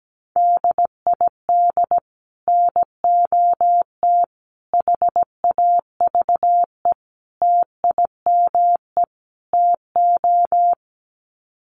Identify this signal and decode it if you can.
Morse code, heard as a keyed tone, about 17 wpm, 700 Hz. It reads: DID NOT HAVE TIME TO
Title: Sets of 5 Words Encore 17wpm